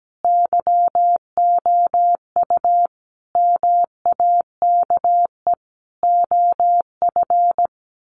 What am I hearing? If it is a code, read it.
Morse code, heard as a keyed tone, about 17 wpm, 700 Hz. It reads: YOU MAKE OF
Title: Sets of 5 Words 17wpm